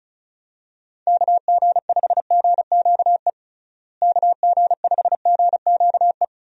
Morse code, heard as a keyed tone, about 34 wpm, 700 Hz. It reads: KG5GQE KG5GQE
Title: USA Call Signs - 6 Characters - Repeated 3x - 35wpm